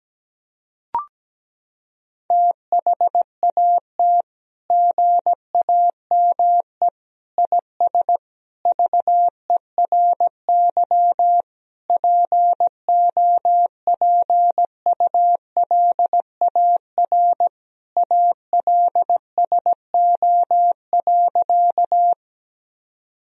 Morse code, heard as a keyed tone, about 17 wpm, 700 Hz. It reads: THAT GAME IS VERY POPULAR ALSO.
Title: Sentences from Top 1000 Words - Review Part 3 - 17wpm